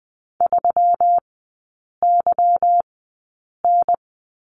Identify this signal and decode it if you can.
Morse code, heard as a keyed tone, about 20 wpm, spaced as for 8 wpm, 700 Hz. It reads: 3YN
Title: 145 Lesson 24e - Letter y - Three Characters 20wpm